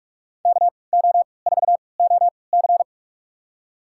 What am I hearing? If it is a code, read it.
Morse code, heard as a keyed tone, about 43 wpm, spaced as for 25 wpm, 700 Hz. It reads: KO4OC
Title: USA Call Signs - 5 Characters - 3x Character Spacing - 45wpm